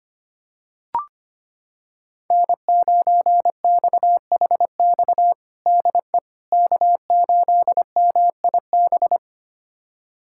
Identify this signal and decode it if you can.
Morse code, heard as a keyed tone, about 25 wpm, 700 Hz. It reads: N9XHX DE K8MIB
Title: USA Callsign DE Callsign - 5 Characters - 25wpm